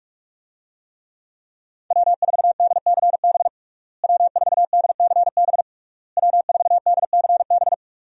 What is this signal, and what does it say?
Morse code, heard as a keyed tone, about 41 wpm, 700 Hz. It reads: W4DCB W4DCB W4DCB
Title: USA Call Signs - 5 Characters - Repeated 3x - 45wpm